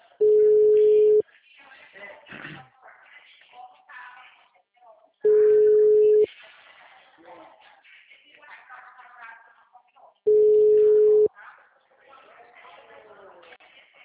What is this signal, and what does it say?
Ringback tone (Europe, 425 Hz), with speech.